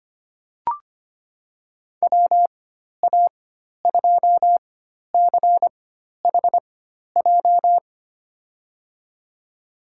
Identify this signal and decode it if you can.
Morse code, heard as a keyed tone, about 25 wpm, spaced as for 12 wpm, 700 Hz. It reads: WA2CHJ
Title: USA Call Signs - 4x Character Spacing - 25wpm